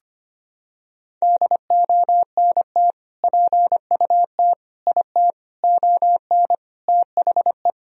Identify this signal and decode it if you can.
Morse code, heard as a keyed tone, about 25 wpm, 700 Hz. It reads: DONT PUT IT ON THE
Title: Sentences from Top 1000 Words - Review Part 2 - 25wpm